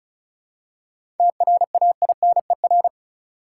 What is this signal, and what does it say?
Morse code, heard as a keyed tone, about 36 wpm, 700 Hz. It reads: TRAINER
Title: Seven Letter Words - 35wpm